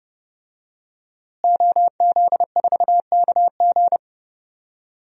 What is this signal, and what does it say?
Morse code, heard as a keyed tone, about 30 wpm, 700 Hz. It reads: OZ4KG
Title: DXCC-100 Call Signs - 5 Characters - Repeated 3x - 30wpm